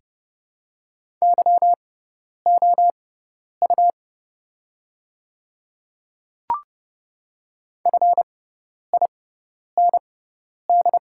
Morse code, heard as a keyed tone, about 30 wpm, spaced as for 10 wpm, 700 Hz. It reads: YOU FIND
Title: 6x Character Spacing - Top 200 Words - 30wpm